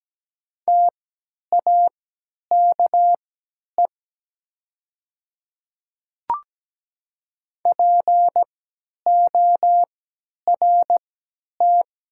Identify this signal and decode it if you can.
Morse code, heard as a keyed tone, about 17 wpm, spaced as for 10 wpm, 700 Hz. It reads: TAKE PORT
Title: 3x Character Spacing - Top 200 Words - 17wpm